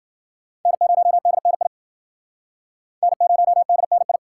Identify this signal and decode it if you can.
Morse code, heard as a keyed tone, about 54 wpm, 700 Hz. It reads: N0DNI N0DNI